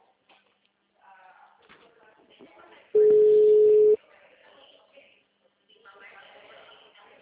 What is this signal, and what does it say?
Ringback tone (Europe, 425 Hz), with speech.